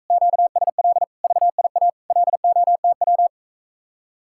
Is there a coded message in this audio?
Yes: Morse code from 0.1 to 3.3 s, heard as a keyed tone, about 45 wpm, 700 Hz. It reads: QSL VIA LOTW